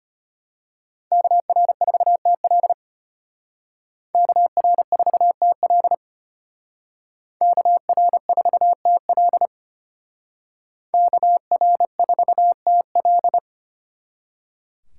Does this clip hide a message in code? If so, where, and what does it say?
Yes: Morse code from 1.1 to 13.4 s, heard as a keyed tone, about 33 wpm, 700 Hz. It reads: KR4TL KR4TL KR4TL KR4TL